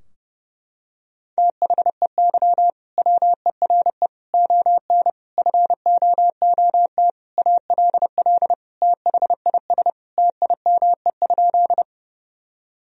Morse code, heard as a keyed tone, about 30 wpm, 700 Hz. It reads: THEY WERE ON FOOT ALL THIS TIME?
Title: Sentences from Top 1000 Words - Review Part 3 - 30wpm